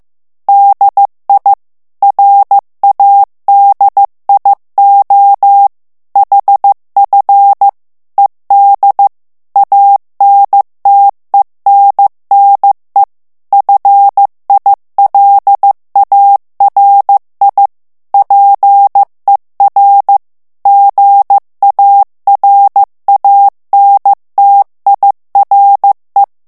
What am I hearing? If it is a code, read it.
Morse code, heard as a keyed tone, about 15 wpm, 800 Hz. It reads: DI RADIO HF ED ANTENNE FILARI PER GARANTIRE
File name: ITALIEN texte clair 1 vitesse 14 mots minute